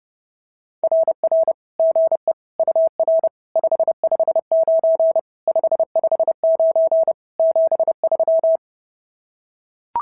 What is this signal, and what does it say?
Morse code, heard as a keyed tone, about 30 wpm, 650 Hz. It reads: RR GE UR 559 559 73
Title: SOTA QSO Practice 30wpm